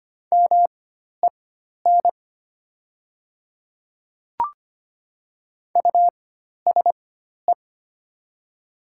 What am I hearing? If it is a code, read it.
Morse code, heard as a keyed tone, about 25 wpm, spaced as for 12 wpm, 700 Hz. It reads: MEN USE
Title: 4x Character Spacing - Top 200 Words - 25wpm